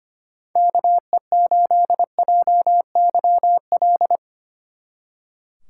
Morse code, heard as a keyed tone, about 32 wpm, 700 Hz. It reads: KE8JYL